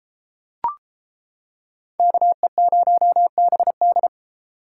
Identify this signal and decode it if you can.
Morse code, heard as a keyed tone, about 33 wpm, 700 Hz. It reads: KE0BD